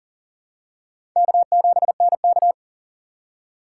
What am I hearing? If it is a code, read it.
Morse code, heard as a keyed tone, about 40 wpm, 700 Hz. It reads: K7NK
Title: USA Call Signs - 4 Characters - 3x Character Spacing - 40wpm